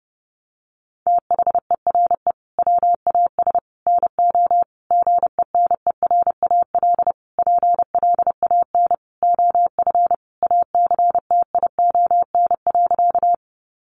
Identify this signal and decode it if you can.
Morse code, heard as a keyed tone, about 30 wpm, 700 Hz. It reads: THERE WAS NO GENERAL PLAN OF ACTION.
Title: Sentences from Top 700 Words 30wpm